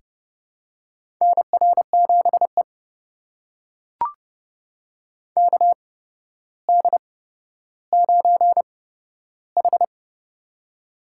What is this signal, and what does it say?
Morse code, heard as a keyed tone, about 30 wpm, spaced as for 13 wpm, 700 Hz. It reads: NR7E KD9H
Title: USA Call Signs - 4 Characters - 8x Character Spacing - 30wpm